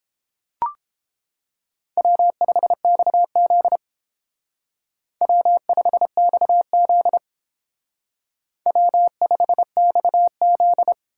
Morse code, heard as a keyed tone, about 30 wpm, 700 Hz. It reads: W5XZ W5XZ W5XZ